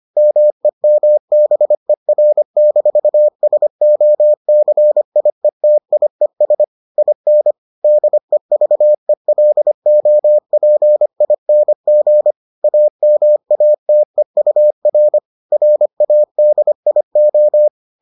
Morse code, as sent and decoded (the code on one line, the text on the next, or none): -- . -- -... .-. -....- ... --- -.-. .. - .. ... / .. -. / -.. ...- . .-.. --- .--. .. -. --. / .- -- .- - . ..- .-. / .-. .- -.. .. ---
MEMBER-SOCIETIES IN DEVELOPING AMATEUR RADIO